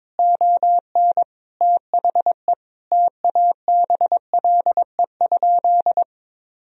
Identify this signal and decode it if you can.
Morse code, heard as a keyed tone, about 21 wpm, 700 Hz. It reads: ON THE TABLE?
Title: Sentences from Top 1000 Words - Review Part 4 - 22wpm